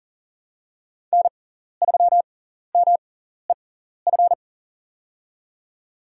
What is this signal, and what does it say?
Morse code, heard as a keyed tone, about 40 wpm, spaced as for 14 wpm, 700 Hz. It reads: N3MEF
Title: USA Call Signs - 5 Characters - 6x Character Spacing - 40wpm